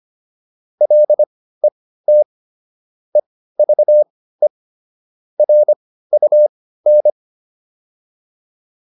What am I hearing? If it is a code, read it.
Morse code, heard as a keyed tone, about 25 wpm, spaced as for 15 wpm, 600 Hz. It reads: LET EVE RUN